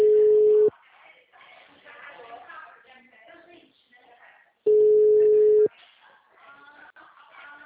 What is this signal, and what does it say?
Ringback tone (Europe, 425 Hz).